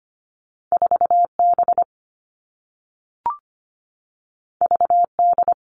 Morse code, heard as a keyed tone, about 25 wpm, 700 Hz. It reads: 4 B V D
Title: All Cut Numbers - 2-Digit 25wpm